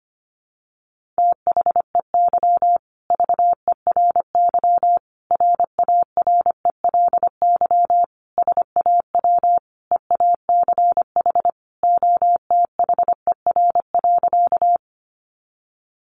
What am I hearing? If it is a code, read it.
Morse code, heard as a keyed tone, about 25 wpm, 700 Hz. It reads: THEY VERY RARELY SAW EACH OTHER.
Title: Sentences from Top 1400 Words 25wpm